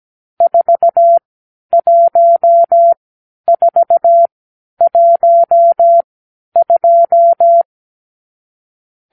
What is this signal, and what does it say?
Morse code, heard as a keyed tone, about 17 wpm, spaced as for 10 wpm, 680 Hz. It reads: 41412
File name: arrl / Review (1-5)